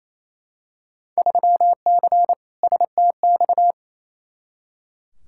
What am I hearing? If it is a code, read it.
Morse code, heard as a keyed tone, about 28 wpm, 700 Hz. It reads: 3C STX
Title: Field Day Exchanges Encore - 28wpm